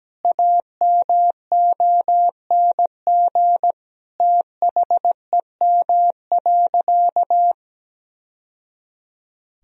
Morse code, heard as a keyed tone, about 17 wpm, 700 Hz. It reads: AMONG THEM.
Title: Sentences from Top 1000 Words - Review Part 2 - 17wpm